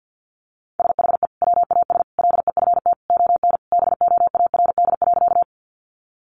Morse code, heard as a keyed tone, about 50 wpm, 700 Hz. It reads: SHE WAS LEFT ON BOARD.
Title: Sentences from Top 700 Words 50wpm